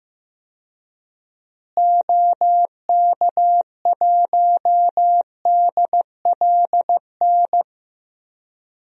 Morse code, heard as a keyed tone, about 15 wpm, 700 Hz. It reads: OK1DLN
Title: DXCC-100 Call Signs - 6 Characters - Repeated 3x - 15wpm